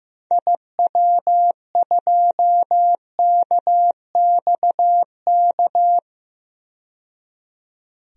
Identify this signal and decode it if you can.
Morse code, heard as a keyed tone, about 15 wpm, 690 Hz. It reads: IW2KXK
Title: DXCC-100 Call Signs - 6 Characters - 15wpm